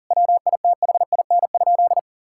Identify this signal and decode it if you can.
Morse code, heard as a keyed tone, about 40 wpm, 700 Hz. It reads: WITHIN?
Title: Sentences from Top 1000 Words - Review Part 6 - 40wpm